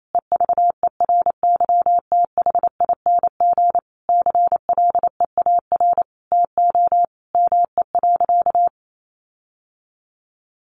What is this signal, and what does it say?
Morse code, heard as a keyed tone, about 28 wpm, 700 Hz. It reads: EVERYTHING CLEAR TO ME.
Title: Sentences from Top 1300 Words 28wpm